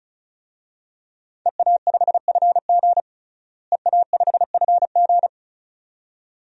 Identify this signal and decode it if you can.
Morse code, heard as a keyed tone, about 34 wpm, 700 Hz. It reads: EA5FG EA5FG
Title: DXCC-100 Call Signs - 5 Characters - Repeated 2x - 35wpm